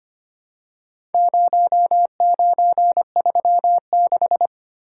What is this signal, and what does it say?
Morse code, heard as a keyed tone, about 25 wpm, 700 Hz. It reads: 0936
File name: practice / 25wpm / Four Letter-Number Combinations - 25wpm